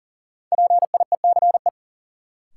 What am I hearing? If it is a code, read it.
Morse code, heard as a keyed tone, about 40 wpm, 700 Hz. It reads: PIECE
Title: Sets of 3 Words - New Word Drill - 40wpm